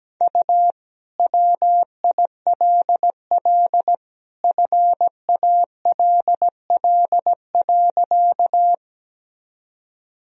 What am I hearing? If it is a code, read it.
Morse code, heard as a keyed tone, about 17 wpm, 700 Hz. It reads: U WILL FALL.
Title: Sentences from Top 500 Words and Abbreviations 17wpm